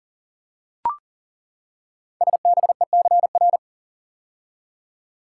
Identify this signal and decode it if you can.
Morse code, heard as a keyed tone, about 38 wpm, 690 Hz. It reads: SBECR
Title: Code Groups - 5 Letters - 40wpm